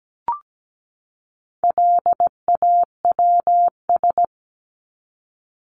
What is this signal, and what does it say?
Morse code, heard as a keyed tone, about 17 wpm, 700 Hz. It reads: LAWS
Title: ES:S Suffix - Top 100 Words with Base Words 17wpm